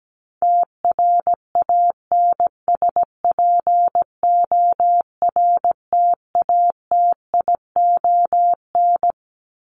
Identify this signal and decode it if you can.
Morse code, heard as a keyed tone, about 17 wpm, 700 Hz. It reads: TRANSPORTATION
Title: Top 1501-1750 Words 17wpm